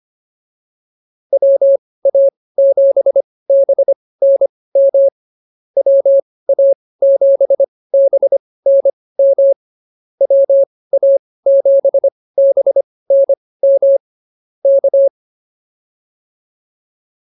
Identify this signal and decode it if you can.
Morse code, heard as a keyed tone, about 24 wpm, spaced as for 18 wpm, 550 Hz. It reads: WA7BNM WA7BNM WA7BNM K
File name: QSO106 18